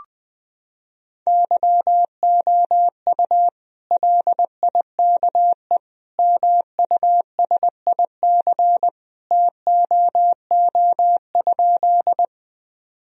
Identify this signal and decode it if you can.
Morse code, heard as a keyed tone, about 20 wpm, 700 Hz. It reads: YOU LIKE MUSIC TOO?
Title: Sentences from Top 1000 Words - Review Part 1 - 20wpm